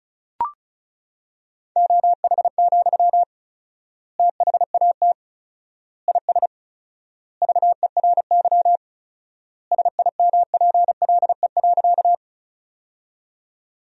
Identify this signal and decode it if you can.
Morse code, heard as a keyed tone, about 34 wpm, 700 Hz. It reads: OH, THAT IS VERY SIMPLE.
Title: 4x Word-Spacing - Sentences from Top 500 Words - 35wpm